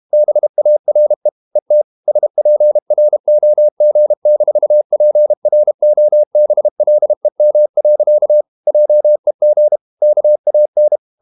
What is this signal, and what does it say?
Morse code, heard as a keyed tone, about 32 wpm, 600 Hz. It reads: BARE ET SPROG-PROBLEM. JEG KAN
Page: Ferie 32 wpm